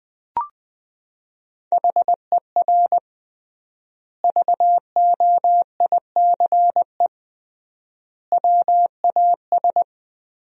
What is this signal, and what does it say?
Morse code, heard as a keyed tone, about 20 wpm, 700 Hz. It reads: HER VOICE WAS
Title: 3x Word-Spacing - Sentences from Top 500 Words - 20wpm